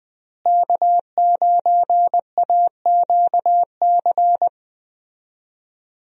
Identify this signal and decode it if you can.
Morse code, heard as a keyed tone, about 20 wpm, 700 Hz. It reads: K9AQC
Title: USA Call Signs - 5 Characters - Repeated 2x - 20wpm